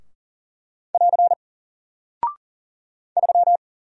Morse code, heard as a keyed tone, about 42 wpm, 700 Hz. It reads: <AR> 3
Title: Single Character (Letter:Number:Punctuation:Prosign) 40wpm